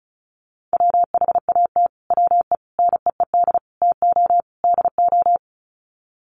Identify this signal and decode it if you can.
Morse code, heard as a keyed tone, about 34 wpm, 700 Hz. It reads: WHAT WE NEED TO DO
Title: Sets of 5 Words Encore 35wpm